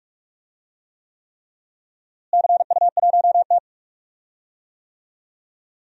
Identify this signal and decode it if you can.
Morse code, heard as a keyed tone, about 45 wpm, 700 Hz. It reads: CU1T
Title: DXCC-100 Call Signs - 4 Characters - Repeated 3x - 45wpm